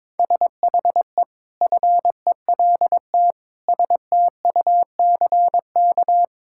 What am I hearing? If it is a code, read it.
Morse code, heard as a keyed tone, about 22 wpm, 700 Hz. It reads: SHE FELT STUCK
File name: practice / 22wpm / Sentences with 8-Word Phrases - 22wpm